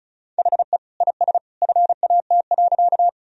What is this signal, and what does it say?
Morse code, heard as a keyed tone, about 34 wpm, 700 Hz. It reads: HE IS FAT.
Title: Sentences from Top 1000 Words - Review Part 4 - 35wpm